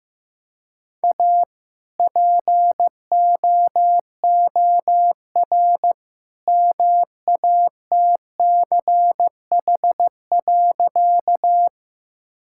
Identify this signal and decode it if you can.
Morse code, heard as a keyed tone, about 15 wpm, 700 Hz. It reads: A POOR MATCH.
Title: Sentences from Top 1600 Words 15wpm